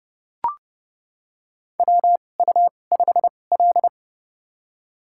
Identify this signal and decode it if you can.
Morse code, heard as a keyed tone, about 30 wpm, spaced as for 22 wpm, 700 Hz. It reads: WU5L
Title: USA Call Signs - 4 Characters - 2x Character Spacing - 30wpm